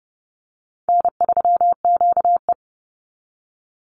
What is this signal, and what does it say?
Morse code, heard as a keyed tone, about 30 wpm, 700 Hz. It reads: N3QE
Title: Easy Call Signs Repeated 3x - 30wpm